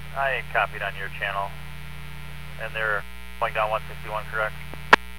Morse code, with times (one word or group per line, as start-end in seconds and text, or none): none